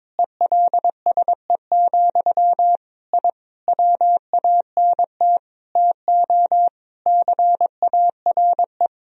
Morse code, heard as a keyed tone, about 22 wpm, 700 Hz. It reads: ELSE, I WANT TO CARE
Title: Sentences from Top 1000 Words - Review Part 2 - 22wpm